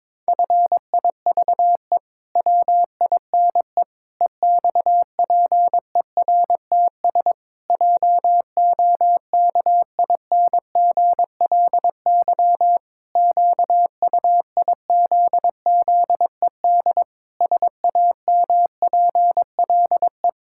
Morse code, heard as a keyed tone, about 22 wpm, 700 Hz. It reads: FIVE WINE EXPERTS JOKINGLY QUIZZED SAMPLE
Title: Pangram Challenge v2 - 22wpm